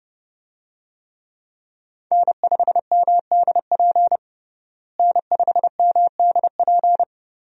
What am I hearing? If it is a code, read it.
Morse code, heard as a keyed tone, about 28 wpm, 700 Hz. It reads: N5MDP N5MDP
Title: USA Call Signs - 5 Characters - Repeated 3x - 30wpm